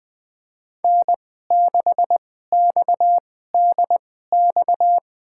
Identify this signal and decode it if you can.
Morse code, heard as a keyed tone, about 20 wpm, spaced as for 15 wpm, 700 Hz. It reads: N6XDX
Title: USA Call Signs - 5 Characters - 2x Character Spacing - 20wpm